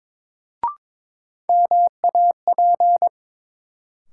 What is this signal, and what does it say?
Morse code, heard as a keyed tone, about 22 wpm, 700 Hz. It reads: MAP
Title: Sentences from Top 400 Words - New Word Drill - 22wpm